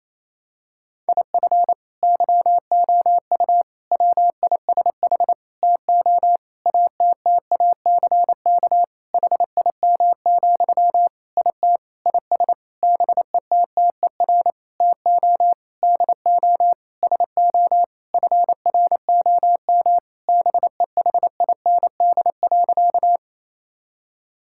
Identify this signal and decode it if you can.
Morse code, heard as a keyed tone, about 28 wpm, 700 Hz. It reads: IF YOU WISH TO ATTACK HIM, IT IS BETTER TO DO SO FROM BEHIND.
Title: Sentences from Top 1600 Words 28wpm